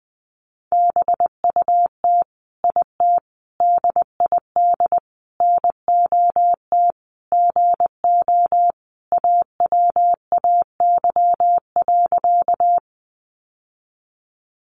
Morse code, heard as a keyed tone, about 20 wpm, 700 Hz. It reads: BUT IT DID NOT GO AWAY.
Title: Sentences from Top 600 Words 20wpm